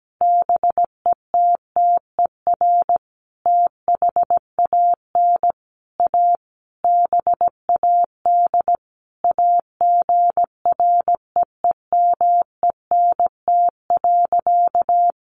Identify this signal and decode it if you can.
Morse code, heard as a keyed tone, about 17 wpm, 700 Hz. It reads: BETTER THAN A BAD AGREEMENT.
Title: Sentences from Top 1500 Words 17wpm